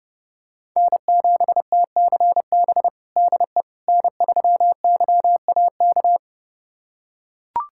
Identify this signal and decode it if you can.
Morse code, heard as a keyed tone, about 30 wpm, 700 Hz. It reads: N7TCB DE N3YAK
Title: USA Callsign DE Callsign - 5 Characters - 30wpm